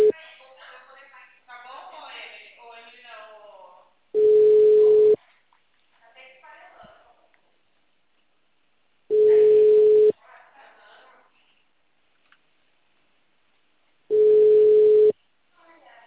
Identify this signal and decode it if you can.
Ringback tone (Europe, 425 Hz), with speech.
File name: out- -2016- - - .96